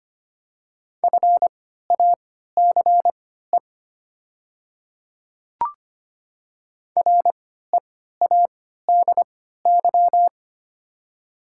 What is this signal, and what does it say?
Morse code, heard as a keyed tone, about 25 wpm, spaced as for 14 wpm, 700 Hz. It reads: FACE READY
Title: 3x Character Spacing - Top 500 Words - 25wpm